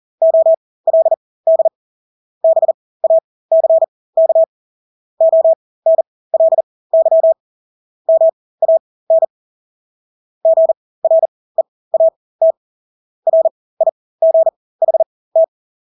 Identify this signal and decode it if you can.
Morse code, heard as a keyed tone, about 40 wpm, spaced as for 20 wpm, 650 Hz. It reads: OLD BACK ONLY MAN GREAT RIGHT
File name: Engelske ord 40-20 wpm